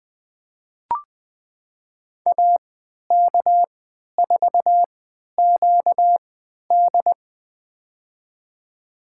Morse code, heard as a keyed tone, about 20 wpm, spaced as for 11 wpm, 700 Hz. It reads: AK4QD
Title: USA Call Signs - 5 Characters - 3x Character Spacing - 20wpm